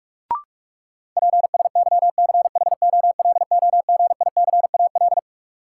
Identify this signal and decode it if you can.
Morse code, heard as a keyed tone, about 46 wpm, 700 Hz. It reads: PSYCHOLOGICAL
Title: Sentences from Top 1400 Words - New Word Drill - 45wpm